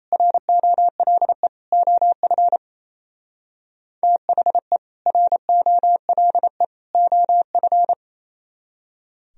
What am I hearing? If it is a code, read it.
Morse code, heard as a keyed tone, about 35 wpm, 700 Hz. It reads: ROLE OF THE ROLE OF